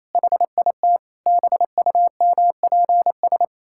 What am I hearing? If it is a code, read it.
Morse code, heard as a keyed tone, about 28 wpm, 700 Hz. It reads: HIT BUMPS
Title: Pangram Challenge v2 - 28wpm